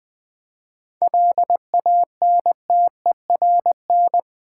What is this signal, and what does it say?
Morse code, heard as a keyed tone, about 20 wpm, 700 Hz. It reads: LANTERN
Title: Easy Words with Hints - 20wpm